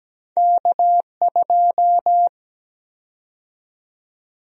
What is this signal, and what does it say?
Morse code, heard as a keyed tone, about 17 wpm, 700 Hz. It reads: K2
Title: USA Call Sign Prefixes - Repeated 2x - 17wpm